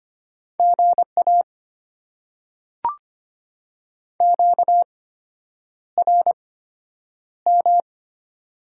Morse code, heard as a keyed tone, about 25 wpm, spaced as for 9 wpm, 700 Hz. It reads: GA QRM
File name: Top 90 QSO Elements - 8x Character Spacing - 25wpm